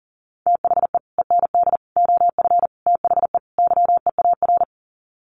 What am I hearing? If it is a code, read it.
Morse code, heard as a keyed tone, about 40 wpm, 700 Hz. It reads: THE END OF THE YEAR
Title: Sets of 5 Words 40wpm